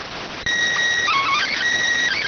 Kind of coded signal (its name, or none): SSTV (slow-scan television)